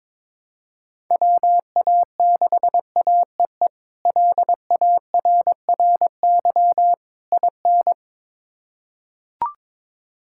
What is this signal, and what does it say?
Morse code, heard as a keyed tone, about 21 wpm, 700 Hz. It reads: WA6AEE LARRY IN
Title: K1USN SST QSO Practice - 22wpm